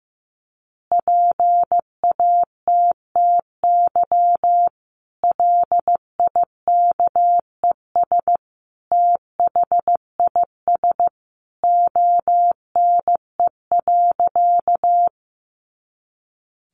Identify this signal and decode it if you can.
Morse code, heard as a keyed tone, about 15 wpm, 700 Hz. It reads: PATTY LIKES THIS ONE.
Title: Additional Easy Phrases 15wpm